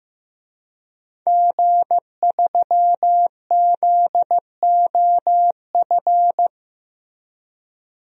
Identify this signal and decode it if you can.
Morse code, heard as a keyed tone, about 15 wpm, 700 Hz. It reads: G3ZOF
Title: DXCC-100 Call Signs - 5 Characters - Repeated 2x - 15wpm